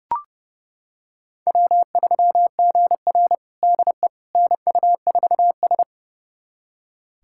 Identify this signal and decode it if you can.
Morse code, heard as a keyed tone, about 30 wpm, 700 Hz. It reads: W3GR DE NU4S
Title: USA Callsign DE Callsign - 4 Characters - 30wpm